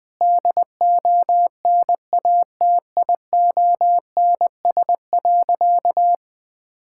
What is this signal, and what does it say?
Morse code, heard as a keyed tone, about 21 wpm, 700 Hz. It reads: DONATIONS.
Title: Sentences with 9-Word Phrases - 20wpm